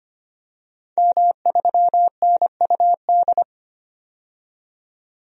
Morse code, heard as a keyed tone, about 25 wpm, 700 Hz. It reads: M3NUD
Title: DXCC-100 Call Signs - 5 Characters - 25wpm